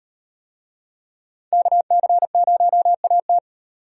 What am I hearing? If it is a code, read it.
Morse code, heard as a keyed tone, about 38 wpm, 700 Hz. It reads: KC0AT